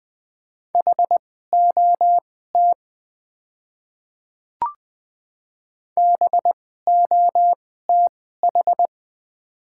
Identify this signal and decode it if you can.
Morse code, heard as a keyed tone, about 20 wpm, spaced as for 14 wpm, 700 Hz. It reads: HOT BOTH
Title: 2x Character Spacing - Top 300 Words - 20wpm